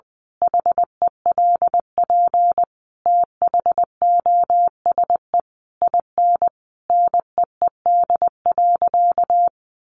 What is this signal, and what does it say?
Morse code, heard as a keyed tone, about 20 wpm, 700 Hz. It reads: HELP THOSE IN NEED.
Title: Sentences from Top 1200 Words 20wpm